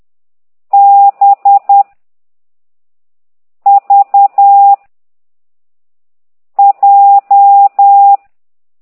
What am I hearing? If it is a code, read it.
Morse code, heard as a keyed tone, about 10 wpm, 800 Hz. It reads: B V J
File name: ct2idw / exerc6